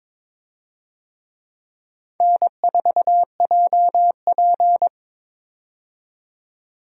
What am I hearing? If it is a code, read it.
Morse code, heard as a keyed tone, about 21 wpm, 700 Hz. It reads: N4JP
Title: USA Call Signs - 4 Characters - 22wpm